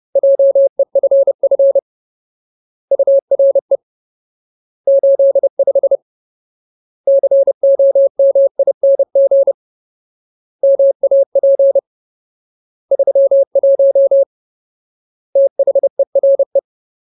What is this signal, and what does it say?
Morse code, heard as a keyed tone, about 30 wpm, 550 Hz. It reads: JEFF URE 85 COMING MAP 31 THERE
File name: Ultimate Word List / 30WPM Ultimate